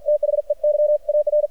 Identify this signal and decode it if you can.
Morse code, heard as a keyed tone, about 53 wpm, 600 Hz, with radio noise. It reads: THEY AR
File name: archived / CRSnet